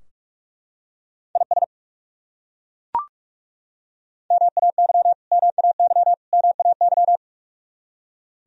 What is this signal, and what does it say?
Morse code, heard as a keyed tone, about 41 wpm, 700 Hz. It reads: IS MAY MAY MAY
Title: Top 100 Words Repeated 45wpm